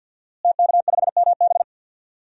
Morse code, heard as a keyed tone, about 48 wpm, 700 Hz. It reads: TK5MB
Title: DXCC-100 Call Signs - 5 Characters - Repeated 3x - 50wpm